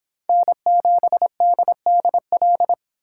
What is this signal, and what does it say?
Morse code, heard as a keyed tone, about 26 wpm, 700 Hz. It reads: N7DDL